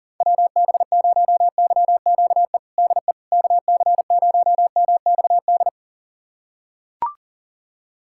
Morse code, heard as a keyed tone, about 38 wpm, 700 Hz. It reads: WB0YQE DE KC0MXD
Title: USA Callsign DE Callsign - 6 Characters - 40wpm